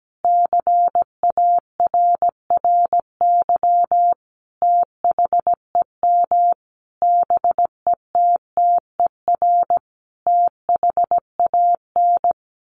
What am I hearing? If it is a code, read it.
Morse code, heard as a keyed tone, about 17 wpm, 700 Hz. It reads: CARRY THEM BETTER THAN
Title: Sentences from Top 1200 Words 17wpm